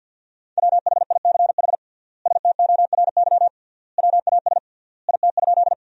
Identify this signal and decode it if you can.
Morse code, heard as a keyed tone, about 50 wpm, 700 Hz. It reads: WHICH STORY WAS IT?
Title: 3x Word-Spacing - Sentences from Top 300 Words - 50wpm